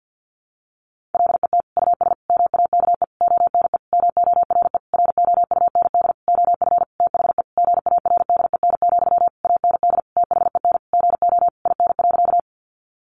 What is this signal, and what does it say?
Morse code, heard as a keyed tone, about 48 wpm, 700 Hz. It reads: LET US MAKE ONE MORE ROUND OF THE GARDEN, AND THEN GO IN.
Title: Sentences from Top 700 Words 50wpm